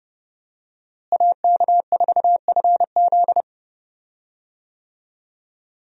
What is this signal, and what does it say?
Morse code, heard as a keyed tone, about 30 wpm, 700 Hz. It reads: AK4FZ
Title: USA Call Signs - 5 Characters - Repeated 2x - 30wpm